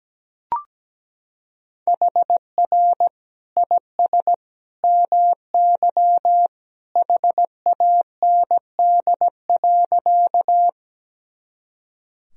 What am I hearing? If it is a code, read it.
Morse code, heard as a keyed tone, about 17 wpm, 700 Hz. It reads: HR IS MY HAND.
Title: Sentences from Top 200 Words and Abbreviations 17wpm